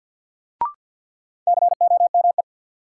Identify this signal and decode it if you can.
Morse code, heard as a keyed tone, about 53 wpm, 700 Hz. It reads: COME
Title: Sets of 2 Words - New Word Drill - 50wpm